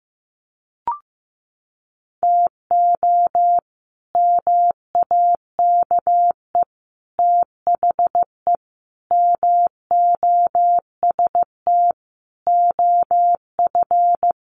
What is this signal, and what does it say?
Morse code, heard as a keyed tone, about 15 wpm, 700 Hz. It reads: TO MAKE THE MOST OF
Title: Sets of 5 Words Encore2 15wpm